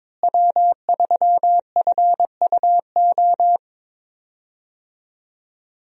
Morse code, heard as a keyed tone, about 22 wpm, 700 Hz. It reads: W3FUO
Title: USA Call Signs - 5 Characters - Repeated 3x - 22wpm